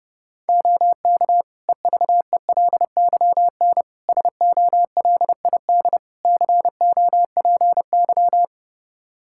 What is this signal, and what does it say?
Morse code, heard as a keyed tone, about 30 wpm, 700 Hz. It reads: OK EVELYN SOLID COPY
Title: Practice QSO - Exchange 3 - More Info - 30wpm